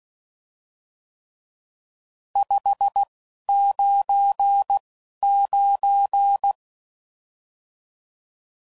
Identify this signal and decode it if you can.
Morse code, heard as a keyed tone, about 16 wpm, spaced as for 11 wpm, 800 Hz. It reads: 599
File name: CW Code Groups @10WPM / RSTs / Various RST No's